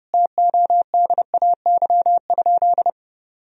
Morse code, heard as a keyed tone, about 31 wpm, 700 Hz. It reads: TODAY?
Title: Sentences from Top 1000 Words - Review Part 4 - 30wpm